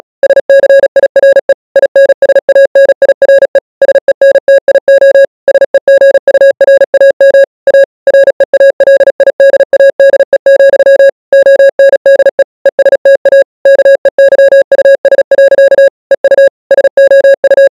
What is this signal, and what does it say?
Morse code, heard as a keyed tone, about 37 wpm, 550 Hz. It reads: SCIRE INSANIRE SENTIO SEGURAM A REALIDADE, ONDE ESTA KEYUS. EU SOU